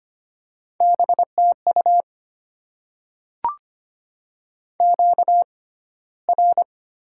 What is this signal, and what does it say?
Morse code, heard as a keyed tone, about 25 wpm, 700 Hz. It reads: B T U Q R
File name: Top 90 QSO Elements - 6x Character Spacing - 25wpm